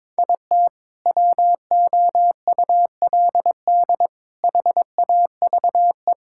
Morse code, heard as a keyed tone, about 22 wpm, 700 Hz. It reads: IT WOULD HAVE
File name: practice / 22wpm / Sentences with 4-Word Phrases - 22wpm